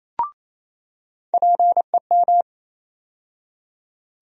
Morse code, heard as a keyed tone, about 27 wpm, 700 Hz. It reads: PEM
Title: USA Call Sign Postfixes - 3 Letters 28wpm